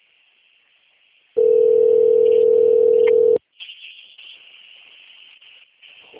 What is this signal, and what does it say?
Ringback tone (North America), 440 + 480 Hz.
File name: out- - - - - .6261